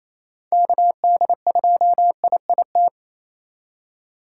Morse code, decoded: KD2IIT

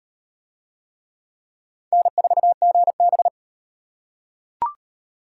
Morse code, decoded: N4GB

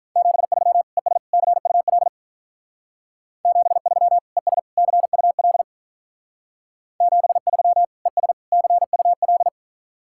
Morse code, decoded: 73 ES CUL 73 ES CUL 73 ES CUL

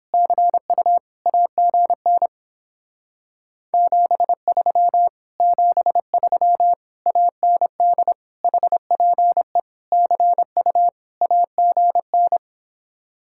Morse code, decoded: CU AGN 73 73 AND HPE CU AGN

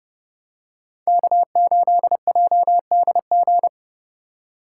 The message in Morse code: K8JDG

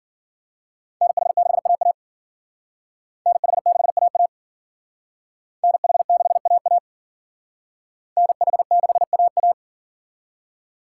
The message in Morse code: NH6AA NH6AA NH6AA NH6AA

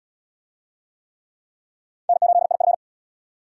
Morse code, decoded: N8IF

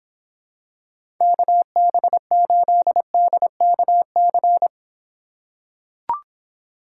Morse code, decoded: KB8DKC

